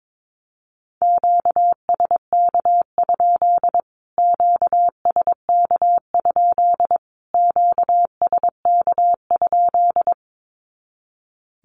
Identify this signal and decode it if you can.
Morse code, heard as a keyed tone, about 22 wpm, 700 Hz. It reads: QSK? QSK? QSK?
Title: Q-Codes Repeated 3x 22wpm